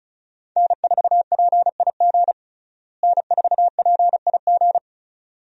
Morse code, heard as a keyed tone, about 36 wpm, 700 Hz. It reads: N4PIG N4PIG
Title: USA Call Signs - 5 Characters - Repeated 2x - 35wpm